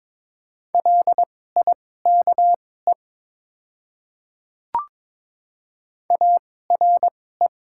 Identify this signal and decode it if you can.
Morse code, heard as a keyed tone, about 21 wpm, spaced as for 16 wpm, 700 Hz. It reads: LIKE ARE
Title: 2x Character Spacing - Top 200 Words - 22wpm